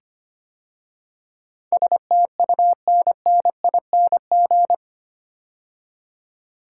Morse code, heard as a keyed tone, about 25 wpm, 700 Hz. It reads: STUNNING
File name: 8 Letter Words - First 3 Letters Given - 25wpm